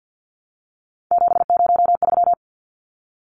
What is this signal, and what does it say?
Morse code, heard as a keyed tone, about 50 wpm, 700 Hz. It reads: 703
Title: Numbers - 3-Digit 50wpm